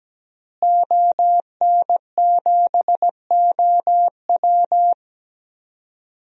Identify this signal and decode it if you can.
Morse code, heard as a keyed tone, about 17 wpm, 700 Hz. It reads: ON7OW
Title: DXCC-100 Call Signs - 5 Characters - Repeated 3x - 17wpm